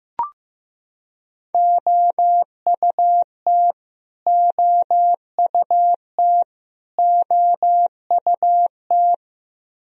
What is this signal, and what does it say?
Morse code, heard as a keyed tone, about 15 wpm, 700 Hz. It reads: OUT OUT OUT
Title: Top 100 Words Repeated 15wpm